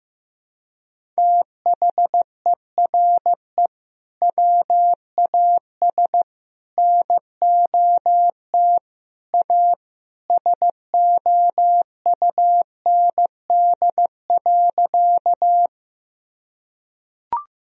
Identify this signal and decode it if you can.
Morse code, heard as a keyed tone, about 15 wpm, 700 Hz. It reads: THERE WAS NOT A SOUND.
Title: Sentences from Top 1000 Words - Review Part 1 - 15wpm